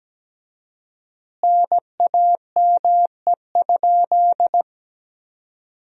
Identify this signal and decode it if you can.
Morse code, heard as a keyed tone, about 17 wpm, 700 Hz. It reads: NAME?
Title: 4x Word-Spacing - Sentences from Top 400 Words - 17wpm